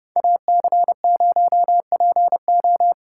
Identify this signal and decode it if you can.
Morse code, heard as a keyed tone, about 29 wpm, 700 Hz. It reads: AC0PO